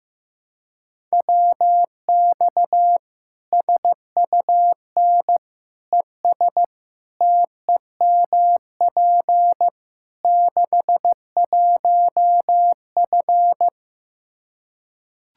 Morse code, heard as a keyed tone, about 15 wpm, 700 Hz. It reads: WX SUN ES TEMP 61F
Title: Practice QSO - Exchange 2 - Info - 15wpm